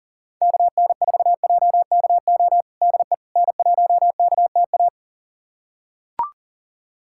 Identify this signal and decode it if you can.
Morse code, heard as a keyed tone, about 40 wpm, 700 Hz. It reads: KN4JKO DE N1KTA
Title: USA Callsign DE Callsign 40wpm